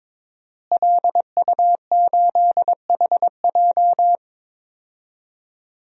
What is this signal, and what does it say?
Morse code, heard as a keyed tone, about 22 wpm, 700 Hz. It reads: LU8HJ
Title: DXCC-100 Call Signs - 5 Characters - Repeated 3x - 22wpm